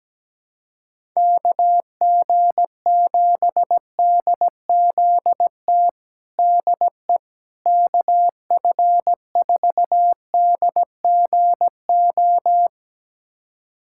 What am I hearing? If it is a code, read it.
Morse code, heard as a keyed tone, about 17 wpm, 700 Hz. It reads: KG7DZT DE KF4DGO
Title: USA Callsign DE Callsign - 6 Characters - 17wpm